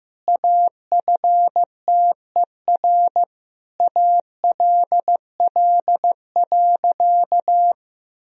Morse code, heard as a keyed tone, about 15 wpm, 700 Hz. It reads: AFTER ALL.
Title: Sentences from Top 1000 Words - Review Part 3 - 15wpm